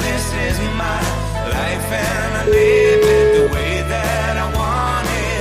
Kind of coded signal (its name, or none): telephone call progress tone